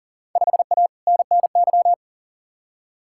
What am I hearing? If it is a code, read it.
Morse code, heard as a keyed tone, about 40 wpm, 700 Hz. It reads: 5A NNY
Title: Field Day Exchanges - 40wpm